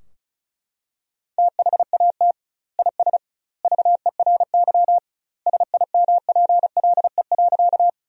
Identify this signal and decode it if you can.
Morse code, heard as a keyed tone, about 36 wpm, 700 Hz. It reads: THAT IS VERY SIMPLE.
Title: 2x Word-Spacing - Sentences from Top 400 Words - 35wpm